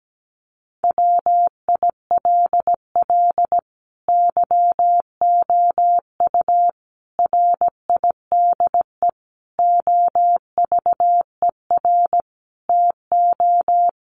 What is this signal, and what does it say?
Morse code, heard as a keyed tone, about 17 wpm, 700 Hz. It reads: WILL YOU RIDE OVER TO
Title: Sentences from Top 1200 Words 17wpm